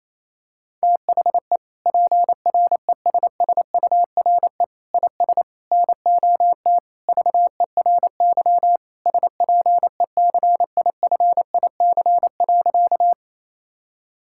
Morse code, heard as a keyed tone, about 28 wpm, 700 Hz. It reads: THE PRESSURE IS NOT VERY SPECIFIC.
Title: Sentences from Top 1000 Words - Review Part 6 - 28wpm